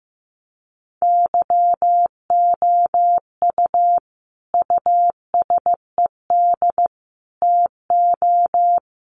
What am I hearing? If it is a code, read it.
Morse code, heard as a keyed tone, about 15 wpm, 700 Hz. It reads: YOU USED TO
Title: Sentences from Top 600 Words 15wpm